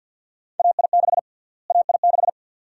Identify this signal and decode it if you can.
Morse code, heard as a keyed tone, about 49 wpm, 700 Hz. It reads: AI6 AI6
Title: USA Call Sign Prefixes - Repeated 2x - 50wpm